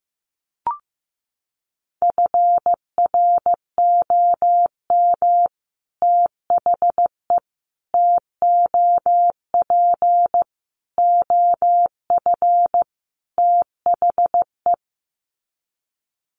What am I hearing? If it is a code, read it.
Morse code, heard as a keyed tone, about 15 wpm, 700 Hz. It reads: FROM THE TOP OF THE
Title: Sets of 5 Words Encore 15wpm